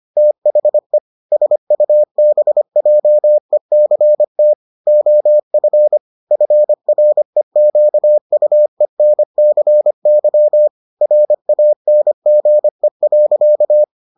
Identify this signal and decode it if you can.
Morse code, heard as a keyed tone, about 25 wpm, 600 Hz. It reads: THE SUBJECT OF FREQUENCY RANGE.